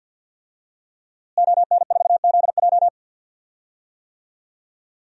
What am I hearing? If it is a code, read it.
Morse code, heard as a keyed tone, about 47 wpm, 700 Hz. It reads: ON4ZJ
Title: DXCC-100 Call Signs - 5 Characters - Repeated 2x - 50wpm